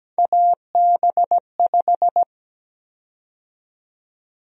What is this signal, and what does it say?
Morse code, heard as a keyed tone, about 17 wpm, 700 Hz. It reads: AB5
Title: USA Call Sign Prefixes - 17wpm